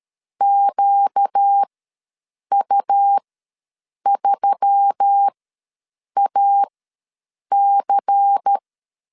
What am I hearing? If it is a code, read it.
Morse code, heard as a keyed tone, about 13 wpm, spaced as for 7 wpm, 800 Hz. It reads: QU3AC
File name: LEZIONI / 6 PKX0 / 6 P ws 12wpm 3117